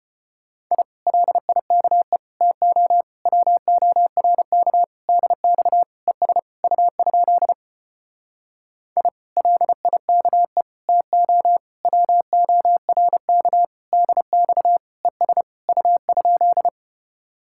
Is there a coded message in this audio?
Yes: Morse code from 0.7 to 16.7 s, heard as a keyed tone, about 33 wpm, 700 Hz. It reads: I LIKE TO WORK DX ES U? I LIKE TO WORK DX ES U?